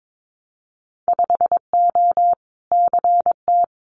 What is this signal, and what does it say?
Morse code, heard as a keyed tone, about 22 wpm, 700 Hz. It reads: 5O CT
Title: Winter Field Day Exchanges Encore 22wpm